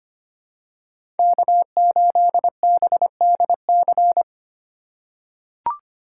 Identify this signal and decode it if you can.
Morse code, heard as a keyed tone, about 25 wpm, 700 Hz. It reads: K8BDC
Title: USA Call Signs - 8x Character Spacing - 25wpm